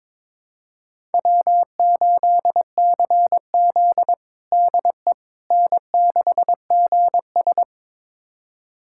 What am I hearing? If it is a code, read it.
Morse code, heard as a keyed tone, about 22 wpm, 700 Hz. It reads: W8CZ DE N6GS
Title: USA Callsign DE Callsign - 4 Characters - 22wpm